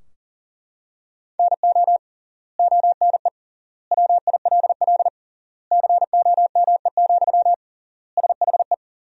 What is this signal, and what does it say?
Morse code, heard as a keyed tone, about 40 wpm, 700 Hz. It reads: NO ONE WILL COME, SHE
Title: 3x Word-Spacing - Sentences from Top 100 Words - 40wpm